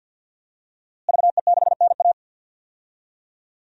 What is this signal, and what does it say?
Morse code, heard as a keyed tone, about 51 wpm, 700 Hz. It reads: VE6NA
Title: DXCC-100 Call Signs - 5 Characters - Repeated 2x - 50wpm